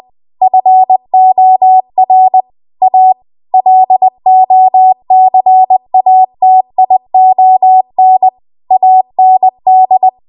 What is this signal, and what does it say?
Morse code, heard as a keyed tone, about 20 wpm, 750 Hz. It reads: FOR A LOCATION AND